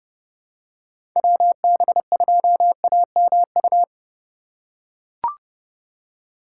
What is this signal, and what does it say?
Morse code, heard as a keyed tone, about 32 wpm, 700 Hz. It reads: WB2AMU
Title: USA Call Signs - 6 Characters - 3x Character Spacing - 30wpm